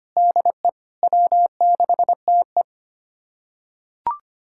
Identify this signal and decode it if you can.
Morse code, heard as a keyed tone, about 25 wpm, 700 Hz. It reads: DE W6TE
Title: USA Callsign DE Callsign - 4 Characters - 25wpm